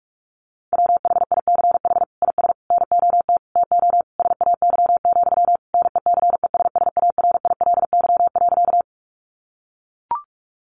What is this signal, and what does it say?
Morse code, heard as a keyed tone, about 44 wpm, 700 Hz. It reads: WHICH IS NOT TO SAY, NECESSARILY.
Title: Sentences from Top 1400 Words 45wpm